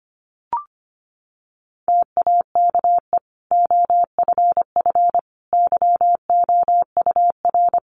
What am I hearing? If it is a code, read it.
Morse code, heard as a keyed tone, about 25 wpm, 700 Hz. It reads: TAKE OFF YOUR
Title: Sentences from Top 1400 Words 25wpm